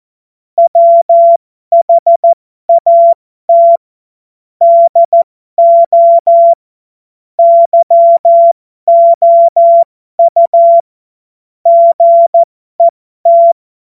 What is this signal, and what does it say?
Morse code, heard as a keyed tone, about 14 wpm, spaced as for 12 wpm, 680 Hz. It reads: WHAT DO YOU GET
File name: Morsekurs / Witze